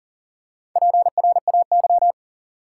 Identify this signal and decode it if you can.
Morse code, heard as a keyed tone, about 37 wpm, 700 Hz. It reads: PRAY
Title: Four Letter Words - 40wpm